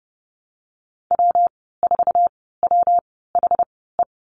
Morse code, heard as a keyed tone, about 30 wpm, spaced as for 17 wpm, 700 Hz. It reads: W4WHE
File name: Lesson 19g - Letter w - Call Signs 30wpm